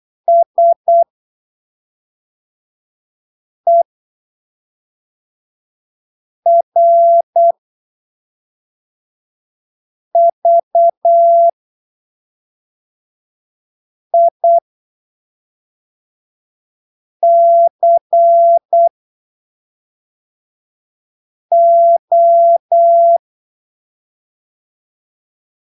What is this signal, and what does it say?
Morse code, heard as a keyed tone, about 8 wpm, spaced as for 3 wpm, 680 Hz. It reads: SERVICO
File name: arquivos / texto / texto 01 08ppm